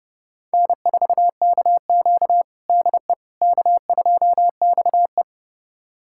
Morse code, heard as a keyed tone, about 30 wpm, 700 Hz. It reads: N4KQ DE K2XE
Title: USA Callsign DE Callsign - 4 Characters - 30wpm